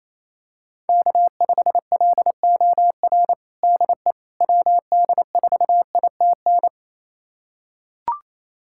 Morse code, heard as a keyed tone, about 28 wpm, 700 Hz. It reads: K5LOR DE WD4ITN
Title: USA Callsign DE Callsign 28wpm